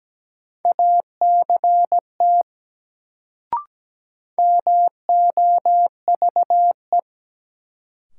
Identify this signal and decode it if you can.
Morse code, heard as a keyed tone, about 17 wpm, 700 Hz. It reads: ACT MOVE
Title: Sentences from Top 200 Words - New Word Drill - 17wpm